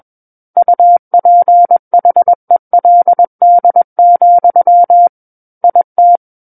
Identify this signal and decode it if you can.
Morse code, heard as a keyed tone, about 21 wpm, 700 Hz. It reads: UPHELD, IT
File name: Book One / 12-Chapter 12 - 21 WPM